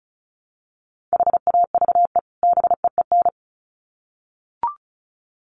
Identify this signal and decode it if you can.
Morse code, heard as a keyed tone, about 34 wpm, 700 Hz. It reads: HAVE BEEN
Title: Sets of 2 Words 35wpm